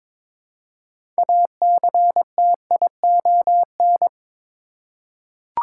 Morse code, heard as a keyed tone, about 22 wpm, 700 Hz. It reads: ACTION
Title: Six Letter Words - 22wpm